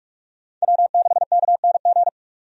Morse code, heard as a keyed tone, about 46 wpm, 700 Hz. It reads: W6KNG
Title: USA Call Signs - 5 Characters - 45wpm